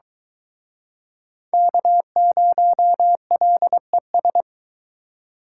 Morse code, heard as a keyed tone, about 24 wpm, 700 Hz. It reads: K0LES